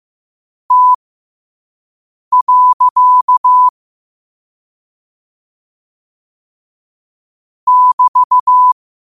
Morse code, heard as a keyed tone, about 15 wpm, spaced as for 5 wpm, 1000 Hz. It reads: T. <BT>
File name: 5wpm